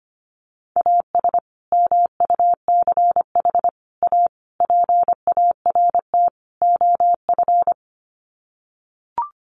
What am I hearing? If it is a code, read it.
Morse code, heard as a keyed tone, about 25 wpm, 700 Hz. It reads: AS MUCH A PART OF
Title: Sets of 5 Words Encore2 25wpm